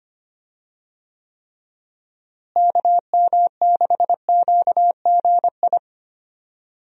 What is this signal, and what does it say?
Morse code, heard as a keyed tone, about 25 wpm, 700 Hz. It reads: KM6QGI
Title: USA Call Signs - 6 Characters - Repeated 2x - 25wpm